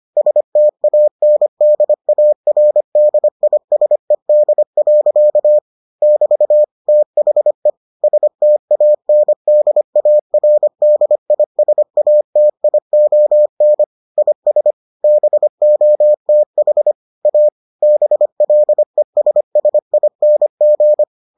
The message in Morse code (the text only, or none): STANDARDISED. <BT> THE STANDARDISATION IS BOTH A BLESSING